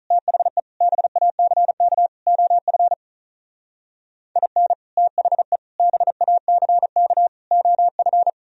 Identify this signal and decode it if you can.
Morse code, heard as a keyed tone, about 44 wpm, 700 Hz. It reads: THE BACK OF IN THE BACK OF
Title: Speed-Racing - Sets of 4 Words 35wpm